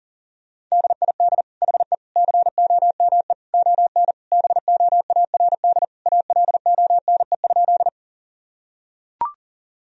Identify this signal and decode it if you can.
Morse code, heard as a keyed tone, about 40 wpm, 700 Hz. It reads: DID HE COME ON BOARD ALONE?
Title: Sentences from Top 1000 Words - Review Part 4 - 40wpm